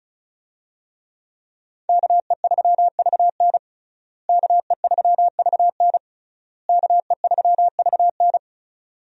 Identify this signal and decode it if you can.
Morse code, heard as a keyed tone, about 36 wpm, 700 Hz. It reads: KE3VN KE3VN KE3VN
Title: USA Call Signs - 5 Characters - Repeated 3x - 35wpm